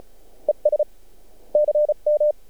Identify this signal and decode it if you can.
Morse code, heard as a keyed tone, about 34 wpm, 600 Hz. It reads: ES CM
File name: archived / CRSnet